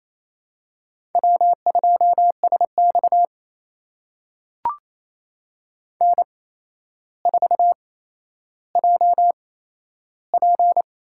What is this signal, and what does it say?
Morse code, heard as a keyed tone, about 28 wpm, spaced as for 12 wpm, 700 Hz. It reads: W2SX N4JP
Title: USA Call Signs - 4 Characters - 8x Character Spacing - 28wpm